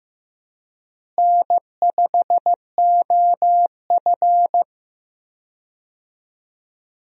Morse code, heard as a keyed tone, about 15 wpm, 700 Hz. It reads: N5OF
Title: USA Call Signs - 4 Characters - Repeated 3x - 15wpm